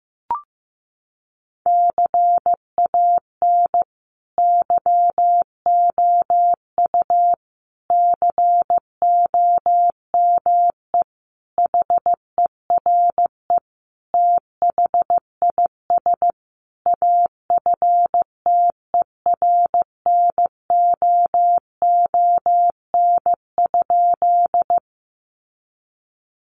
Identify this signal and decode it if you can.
Morse code, heard as a keyed tone, about 15 wpm, 700 Hz. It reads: CAN YOU COME HERE THIS AFTERNOON?
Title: Sentences from Top 1500 Words 15wpm